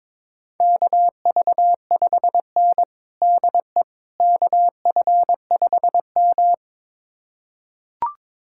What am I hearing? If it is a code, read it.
Morse code, heard as a keyed tone, about 22 wpm, 700 Hz. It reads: KV5N DE KF5M
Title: USA Callsign DE Callsign - 4 Characters - 22wpm